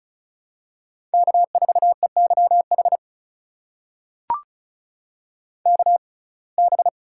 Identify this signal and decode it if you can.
Morse code, heard as a keyed tone, about 36 wpm, spaced as for 25 wpm, 700 Hz. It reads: K4EYH KB
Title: USA Call Signs - 5 Characters - 6x Character Spacing - 35wpm